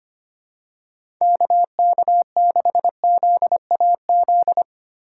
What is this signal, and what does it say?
Morse code, heard as a keyed tone, about 25 wpm, 700 Hz. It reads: KK6ZAZ
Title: USA Call Signs - 6 Characters - 2x Character Spacing - 25wpm